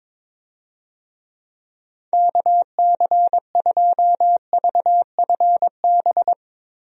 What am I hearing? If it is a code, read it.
Morse code, heard as a keyed tone, about 22 wpm, 700 Hz. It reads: KC2VFB
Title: USA Call Signs - 6 Characters - Repeated 2x - 22wpm